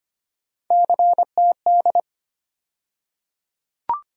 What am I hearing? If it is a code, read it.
Morse code, heard as a keyed tone, about 25 wpm, 700 Hz. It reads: CTD
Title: USA Call Sign Postfixes - 3 Letters 25wpm